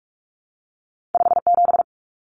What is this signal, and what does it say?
Morse code, heard as a keyed tone, about 46 wpm, 700 Hz. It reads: 57
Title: Common Cut Numbers - 2-Digit 45wpm